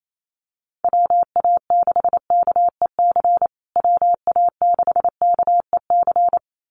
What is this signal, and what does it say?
Morse code, heard as a keyed tone, about 28 wpm, 700 Hz. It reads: WA6KEC WA6KEC
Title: USA Call Signs Repeated 2x 28wpm